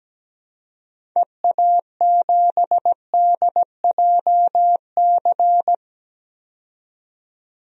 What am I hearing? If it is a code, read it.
Morse code, heard as a keyed tone, about 17 wpm, 700 Hz. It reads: EA7DJC
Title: DXCC-100 Call Signs - 6 Characters - 17wpm